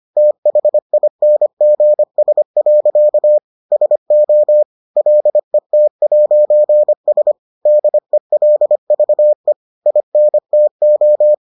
Morse code, heard as a keyed tone, about 25 wpm, 600 Hz. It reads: THINGS. SO LET'S DELVE INTO